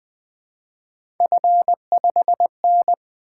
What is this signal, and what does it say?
Morse code, heard as a keyed tone, about 20 wpm, 700 Hz. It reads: F5N
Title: Three Letter-Number Combinations - 20wpm